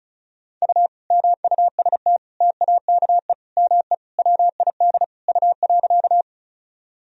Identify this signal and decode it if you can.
Morse code, heard as a keyed tone, about 36 wpm, 700 Hz. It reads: U MUST TAKE ME WID U.
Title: Sentences from Top 200 Words and Abbreviations 35wpm